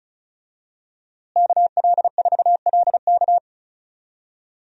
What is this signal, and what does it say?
Morse code, heard as a keyed tone, about 36 wpm, 700 Hz. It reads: KL4LK
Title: USA Call Signs - 5 Characters - Repeated 3x - 35wpm